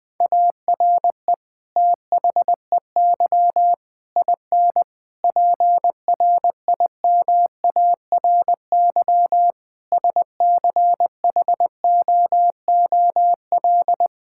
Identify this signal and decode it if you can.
Morse code, heard as a keyed tone, about 20 wpm, 700 Hz. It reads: ARE THEY IN PRIMARY SCHOOL
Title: Sentences from Top 1000 Words - Review Part 5 - 20wpm